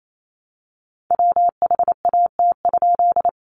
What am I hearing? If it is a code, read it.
Morse code, heard as a keyed tone, about 28 wpm, 700 Hz. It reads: WHAT?
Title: A Sound Of Thunder 28wpm